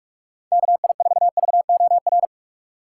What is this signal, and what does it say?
Morse code, heard as a keyed tone, about 46 wpm, 700 Hz. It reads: KI4VOR
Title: USA Call Signs - 4x Character Spacing - 45wpm